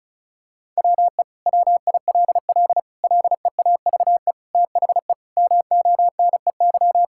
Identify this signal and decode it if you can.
Morse code, heard as a keyed tone, about 36 wpm, 700 Hz. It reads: WE WILL LEAVE THE MONEY
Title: Sentences from Top 1000 Words - Review Part 6 - 35wpm